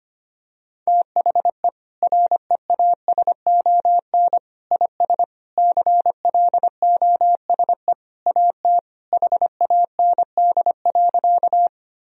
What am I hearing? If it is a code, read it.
Morse code, heard as a keyed tone, about 25 wpm, 700 Hz. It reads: THE REASON IS CLOSE AT HAND.
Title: Sentences from Top 1000 Words - Review Part 1 - 25wpm